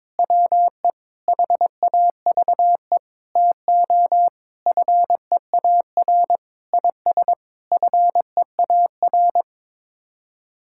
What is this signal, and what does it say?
Morse code, heard as a keyed tone, about 22 wpm, 700 Hz. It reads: WE HAVE TO FEAR IS FEAR
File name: Sets of 9 Words - 22wpm